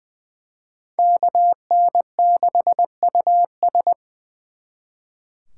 Morse code, heard as a keyed tone, about 24 wpm, 700 Hz. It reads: KN6US
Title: Speed Racing - USA 2x2 Call Signs 20wpm